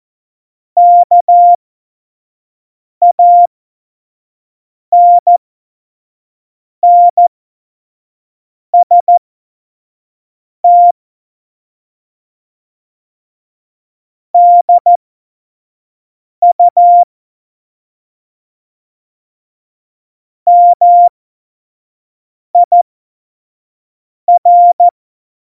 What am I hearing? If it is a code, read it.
Morse code, heard as a keyed tone, about 14 wpm, spaced as for 5 wpm, 700 Hz. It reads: KANNST DU MIR
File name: Morsekurs / Witze / Witz 16. April 2019, 25 F70